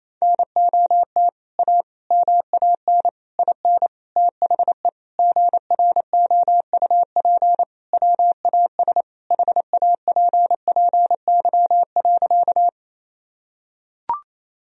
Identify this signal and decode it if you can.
Morse code, heard as a keyed tone, about 28 wpm, 700 Hz. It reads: NOT A MAN IN THE GROUP WAS HAPPY.
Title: Sentences from Top 1000 Words - Review Part 1 - 28wpm